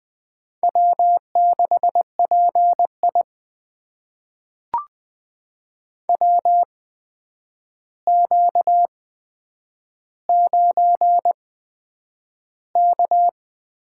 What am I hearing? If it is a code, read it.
Morse code, heard as a keyed tone, about 20 wpm, spaced as for 9 wpm, 700 Hz. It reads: W6PI WQ9K
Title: USA Call Signs - 4 Characters - 8x Character Spacing - 20wpm